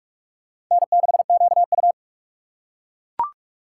Morse code, heard as a keyed tone, about 41 wpm, 700 Hz. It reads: N6QU
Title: USA Call Signs - 4 Characters - 6x Character Spacing - 45wpm